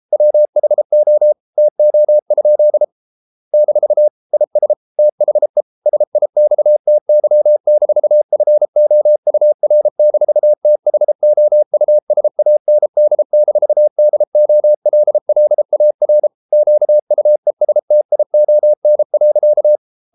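Morse code, heard as a keyed tone, about 33 wpm, 600 Hz. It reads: WHO TO? - IS THE SIXTY-FOUR-THOUSAND-DOLLAR QUESTION.
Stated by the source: Morse code practice at 32 wpm from CW med Gnister, reading Ken Follett's novel Never in English.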